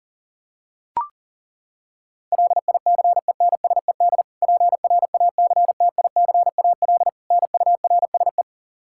Morse code, heard as a keyed tone, about 40 wpm, 700 Hz. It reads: LICENSED PRACTICAL NURSE
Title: Practice QSO - Exchange 3 - More Info - 40wpm